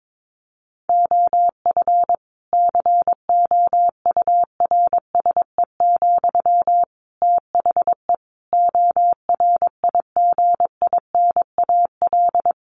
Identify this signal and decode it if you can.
Morse code, heard as a keyed tone, about 22 wpm, 700 Hz. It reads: OF COURSE, THE ORIGINAL
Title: Sentences from Top 1500 Words 22wpm